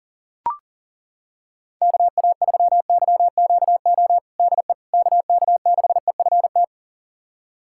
Morse code, heard as a keyed tone, about 40 wpm, 700 Hz. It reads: KA3YQO DE KK6EFT
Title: USA Callsign DE Callsign - 6 Characters - 40wpm